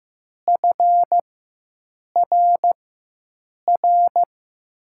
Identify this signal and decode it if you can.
Morse code, heard as a keyed tone, about 15 wpm, spaced as for 7 wpm, 700 Hz. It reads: FRR
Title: USA Call Signs - 6 Characters - 4x Character Spacing - 15wpm